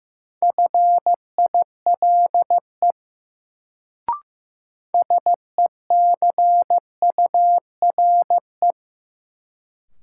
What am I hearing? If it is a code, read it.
Morse code, heard as a keyed tone, about 15 wpm, 700 Hz. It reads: FILE SECURE
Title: Sentences from Top 1200 Words - New Word Drill Alt - 15wpm